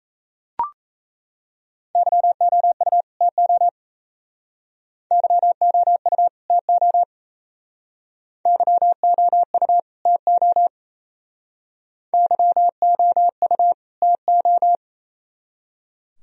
Morse code, heard as a keyed tone, about 34 wpm, 700 Hz. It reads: YOU TO YOU TO YOU TO YOU TO
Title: Speed-Racing - Sets of 2 Words 28wpm